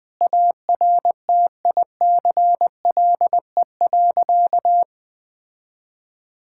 Morse code, heard as a keyed tone, about 20 wpm, 700 Hz. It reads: ARTICLE.
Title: Sentences from Top 1000 Words - Review Part 5 - 20wpm